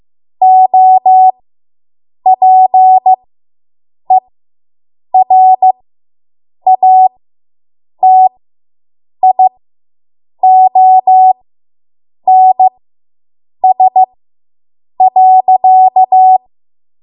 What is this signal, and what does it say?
Morse code, heard as a keyed tone, about 15 wpm, spaced as for 7 wpm, 750 Hz. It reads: OPERATIONS.